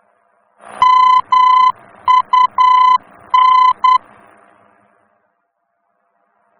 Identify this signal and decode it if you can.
Morse code, heard as a keyed tone, about 10 wpm, 1010 Hz. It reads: MUN
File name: germany / mx / MUN-D-VOR